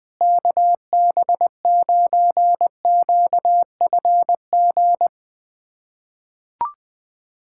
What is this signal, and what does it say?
Morse code, heard as a keyed tone, about 21 wpm, 700 Hz. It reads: KB9QFG
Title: USA Callsign DE Callsign - 6 Characters - 20wpm